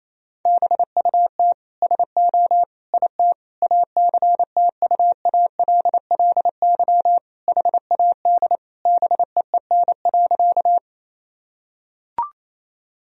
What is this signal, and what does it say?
Morse code, heard as a keyed tone, about 28 wpm, 700 Hz. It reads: BUT SO IT ACTUALLY HAD BEEN.
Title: Sentences from Top 1000 Words - Review Part 4 - 28wpm